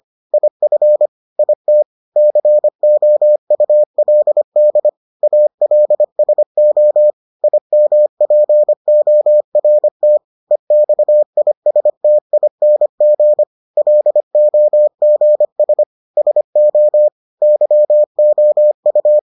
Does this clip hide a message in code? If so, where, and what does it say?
Yes: Morse code from 0.3 to 19.2 s, heard as a keyed tone, about 25 wpm, 600 Hz. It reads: IF IT COULD ALSO IMPORT EXISTING LOGS SO YOU